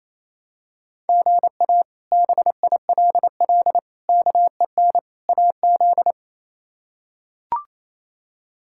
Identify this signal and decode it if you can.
Morse code, heard as a keyed tone, about 28 wpm, 700 Hz. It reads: GA BILL KEN AZ
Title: K1USN SST QSO Practice - 28wpm